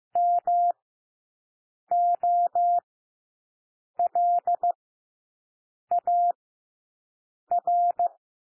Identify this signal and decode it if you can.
Morse code, heard as a keyed tone, about 15 wpm, spaced as for 6 wpm, 700 Hz. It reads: MOLAR
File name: cwmp3 / W1 SOUND WORDS 15 wpm